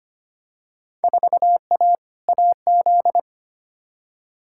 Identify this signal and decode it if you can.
Morse code, heard as a keyed tone, about 25 wpm, 700 Hz. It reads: 4A AZ
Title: Field Day Exchanges - 25wpm